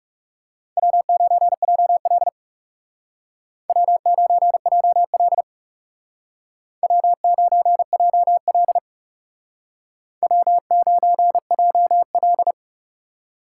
Morse code, heard as a keyed tone, about 40 wpm, 700 Hz. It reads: W9JL W9JL W9JL W9JL